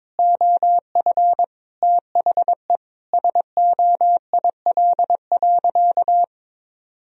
Morse code, heard as a keyed tone, about 22 wpm, 700 Hz. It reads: OF THE SOIL.
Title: Sentences from Top 1000 Words - Review Part 4 - 22wpm